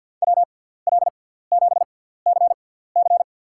Morse code, heard as a keyed tone, about 50 wpm, spaced as for 17 wpm, 700 Hz. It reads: WL7CC
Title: USA Call Signs - 6x Character Spacing - 50wpm